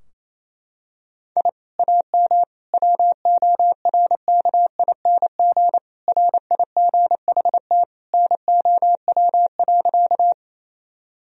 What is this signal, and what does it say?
Morse code, heard as a keyed tone, about 28 wpm, 700 Hz. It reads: I AM WORKING RIGHT NOW.
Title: Sentences from Top 1000 Words - Review Part 4 - 28wpm